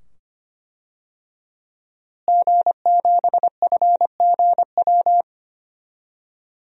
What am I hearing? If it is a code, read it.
Morse code, heard as a keyed tone, about 25 wpm, 700 Hz. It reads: G7FGW
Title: DXCC-100 Call Signs - 5 Characters - Repeated 2x - 25wpm